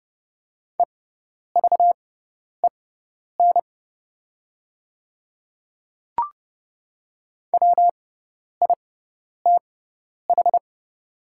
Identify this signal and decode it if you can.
Morse code, heard as a keyed tone, about 30 wpm, spaced as for 10 wpm, 700 Hz. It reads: EVEN WITH
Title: 6x Character Spacing - Top 200 Words - 30wpm